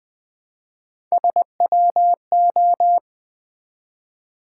Morse code, heard as a keyed tone, about 20 wpm, 700 Hz. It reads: SWO
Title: Build A Word - 1 Letter At A Time - 5-Letter Words - 20wpm